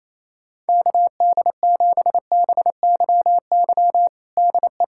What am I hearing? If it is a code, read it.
Morse code, heard as a keyed tone, about 28 wpm, 700 Hz. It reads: KD7BYY DE
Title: USA Callsign DE Callsign - 6 Characters - 28wpm